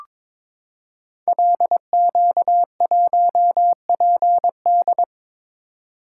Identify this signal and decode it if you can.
Morse code, heard as a keyed tone, about 22 wpm, 700 Hz. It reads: LQ1PD
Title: Code Groups - 5 Letter:Numbers - 22wpm